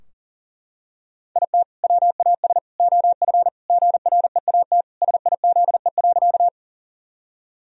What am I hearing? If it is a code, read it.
Morse code, heard as a keyed tone, about 40 wpm, 700 Hz. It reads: IT WAS OF GREAT SIZE.
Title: Sentences from Top 1000 Words - Review Part 2 - 40wpm